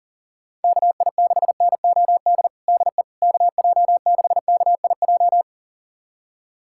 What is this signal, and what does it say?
Morse code, heard as a keyed tone, about 37 wpm, 700 Hz. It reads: KI6NOD DE KJ6KIJ
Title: USA Callsign DE Callsign - 6 Characters - 40wpm